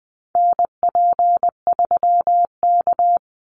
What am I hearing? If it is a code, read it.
Morse code, heard as a keyed tone, about 20 wpm, 700 Hz. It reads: NP3K
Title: Call Signs Repeated 3x - 20wpm